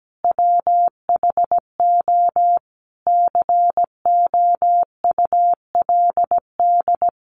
Morse code, heard as a keyed tone, about 17 wpm, 700 Hz. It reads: WHO COULD
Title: Sentences from Top 1200 Words 17wpm